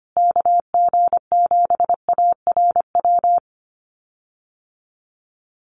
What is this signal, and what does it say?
Morse code, heard as a keyed tone, about 25 wpm, 700 Hz. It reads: KG7ARW
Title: USA Call Signs Repeated 3x 25wpm